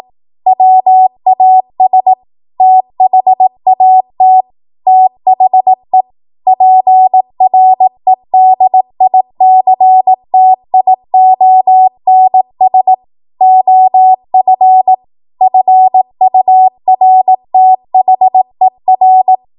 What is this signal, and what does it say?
Morse code, heard as a keyed tone, about 18 wpm, 750 Hz. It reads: WAS THAT THE PREDICTIONS OF FURTHER
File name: Archive / 18 WPM / 18WPM